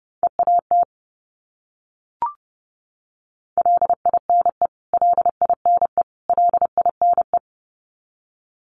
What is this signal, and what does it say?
Morse code, heard as a keyed tone, about 30 wpm, 700 Hz. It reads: EAT LINE LINE LINE
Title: Top 500 Words Repeated 30wpm